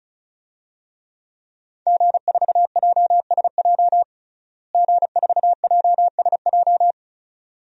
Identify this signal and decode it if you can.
Morse code, heard as a keyed tone, about 34 wpm, 700 Hz. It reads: G4JSJ G4JSJ
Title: DXCC-100 Call Signs - 5 Characters - Repeated 2x - 35wpm